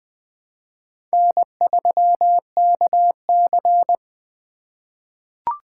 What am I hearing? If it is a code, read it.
Morse code, heard as a keyed tone, about 20 wpm, 700 Hz. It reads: N3KC
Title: Four Letter-Number Combinations - 20wpm